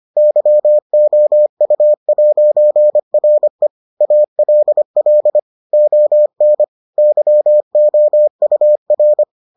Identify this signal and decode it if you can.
Morse code, heard as a keyed tone, about 25 wpm, 600 Hz. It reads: YOU'RE ALL ON YOUR